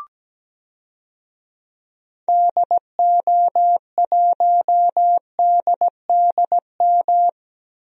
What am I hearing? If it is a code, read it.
Morse code, heard as a keyed tone, about 17 wpm, 700 Hz. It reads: DO1DDM
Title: DXCC-100 Call Signs - 6 Characters - 17wpm